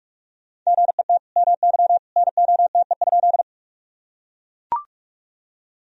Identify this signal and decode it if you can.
Morse code, heard as a keyed tone, about 41 wpm, 700 Hz. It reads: GET MY NOTE?
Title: Sentences from Top 1000 Words - Review Part 2 - 45wpm